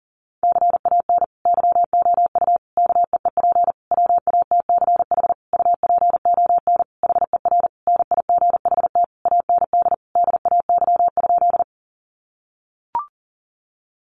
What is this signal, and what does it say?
Morse code, heard as a keyed tone, about 40 wpm, 700 Hz. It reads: CAN YOU KEEP WATCH UPON HER NIGHT AND DAY?
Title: Sentences from Top 1100 Words 40wpm